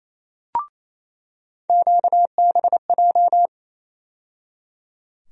Morse code, heard as a keyed tone, about 28 wpm, 700 Hz. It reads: QBJ
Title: USA Call Sign Postfixes - 3 Letters 28wpm